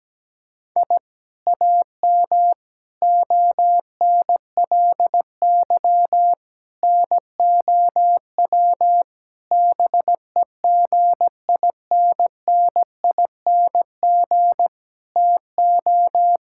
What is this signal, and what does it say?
Morse code, heard as a keyed tone, about 17 wpm, 700 Hz. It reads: I AM ONLY NOW BEGINNING TO
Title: Sentences from Top 1000 Words - Review Part 5 - 17wpm